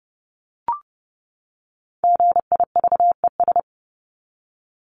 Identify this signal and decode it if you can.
Morse code, heard as a keyed tone, about 30 wpm, 700 Hz. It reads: GIVES
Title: ES:S Suffix - Top 100 Words with Base Words 30wpm